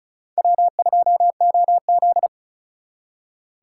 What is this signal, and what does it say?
Morse code, heard as a keyed tone, about 43 wpm, 700 Hz. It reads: W2OZ
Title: Speed Racing - USA 1x2 Call Signs 35wpm